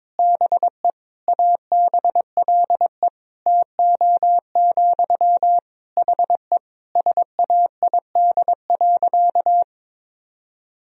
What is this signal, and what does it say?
Morse code, heard as a keyed tone, about 22 wpm, 700 Hz. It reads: BE ABLE TO, HE SAID.
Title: Sentences from Top 1000 Words - Review Part 2 - 22wpm